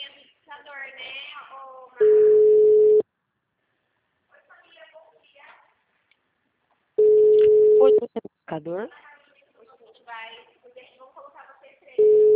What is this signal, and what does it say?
Ringback tone (Europe, 425 Hz), with speech.